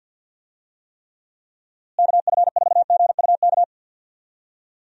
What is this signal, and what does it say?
Morse code, heard as a keyed tone, about 51 wpm, 700 Hz. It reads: KF4GUK
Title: USA Call Signs - 6 Characters - Repeated 2x - 50wpm